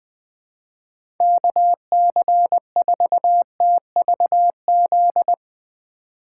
Morse code, heard as a keyed tone, about 20 wpm, 700 Hz. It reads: KC4TVZ
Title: 2023 POTA Activator Call Signs - 20wpm